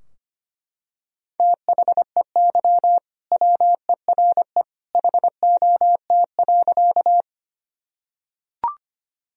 Morse code, heard as a keyed tone, about 25 wpm, 700 Hz. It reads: THEY WERE HOT.
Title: Sentences from Top 1000 Words - Review Part 3 - 25wpm